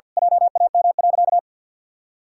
Morse code, heard as a keyed tone, about 52 wpm, 700 Hz. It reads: JAM.
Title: Pangram Challenge v2 - 50wpm